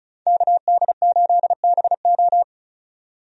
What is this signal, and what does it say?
Morse code, heard as a keyed tone, about 34 wpm, 700 Hz. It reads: KD8BO
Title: USA Call Signs - 5 Characters - 35wpm